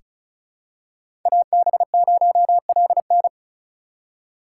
Morse code, heard as a keyed tone, about 34 wpm, 700 Hz. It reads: AB0LN
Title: USA Call Signs - 5 Characters - 4x Character Spacing - 35wpm